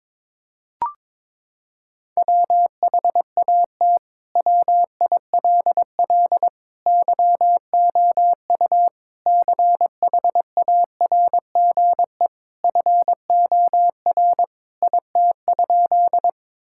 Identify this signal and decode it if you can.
Morse code, heard as a keyed tone, about 22 wpm, 700 Hz. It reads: WHAT WILL YOU CHARGE FOR IT?
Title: Sentences from Top 1000 Words - Review Part 6 - 22wpm